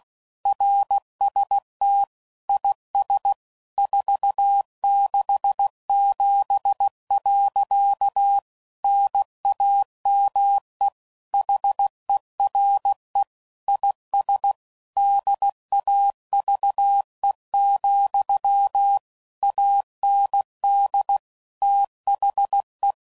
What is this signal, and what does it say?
Morse code, heard as a keyed tone, about 16 wpm, 800 Hz. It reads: RST IS 467. NAME HERE IS DAVE, AND THE
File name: CW Code Groups @15WPM / QSOs / WA7QOX QRX